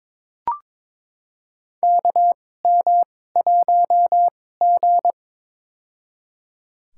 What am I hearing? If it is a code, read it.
Morse code, heard as a keyed tone, about 22 wpm, spaced as for 16 wpm, 700 Hz. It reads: KM1G
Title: USA Call Signs - 4 Characters - 2x Character Spacing - 22wpm